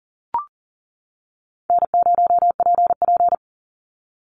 Morse code, heard as a keyed tone, about 40 wpm, 700 Hz. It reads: N0PP
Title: Call Signs - 40wpm